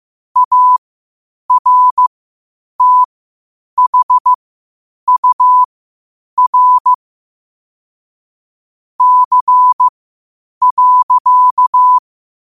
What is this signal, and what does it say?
Morse code, heard as a keyed tone, about 15 wpm, spaced as for 8 wpm, 1000 Hz. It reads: ARTHUR C.
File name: qotdfiles / 8wpm